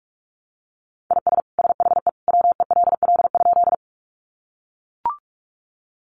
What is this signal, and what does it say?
Morse code, heard as a keyed tone, about 44 wpm, 700 Hz. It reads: IS SHE WELL?
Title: Sentences from Top 200 Words 45wpm